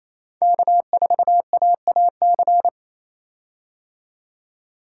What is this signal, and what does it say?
Morse code, heard as a keyed tone, about 28 wpm, 700 Hz. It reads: K4AAC
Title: K1USN SST QSO Practice - 28wpm